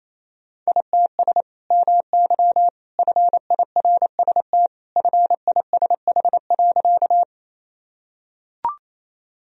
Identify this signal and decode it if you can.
Morse code, heard as a keyed tone, about 28 wpm, 700 Hz. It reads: ITS MY FIRST FISH.
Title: Sentences from Top 1000 Words - Review Part 1 - 28wpm